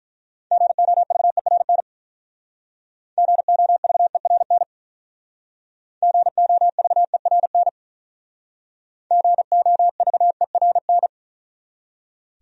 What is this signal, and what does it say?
Morse code, heard as a keyed tone, about 45 wpm, 700 Hz. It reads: GOVERN GOVERN GOVERN GOVERN